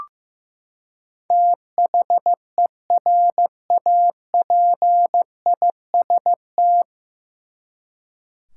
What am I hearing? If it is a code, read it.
Morse code, heard as a keyed tone, about 15 wpm, 700 Hz. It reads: THERAPIST
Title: Practice QSO - Exchange 3 - More Info - 15wpm